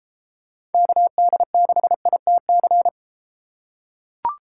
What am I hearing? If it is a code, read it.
Morse code, heard as a keyed tone, about 32 wpm, 700 Hz. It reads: KD6ITC